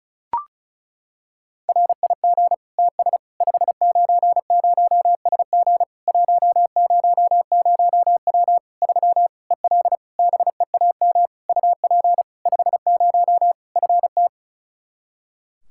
Morse code, heard as a keyed tone, about 36 wpm, 700 Hz. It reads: RIG TS 590SG 100W 3 EL BEAM UP 50 FT
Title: K1USN SST QSO Practice 35wpm